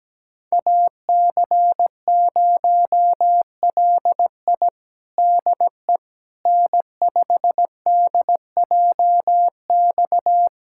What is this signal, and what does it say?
Morse code, heard as a keyed tone, about 17 wpm, 700 Hz. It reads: AC0LI DE N5DJX
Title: USA Callsign DE Callsign 17wpm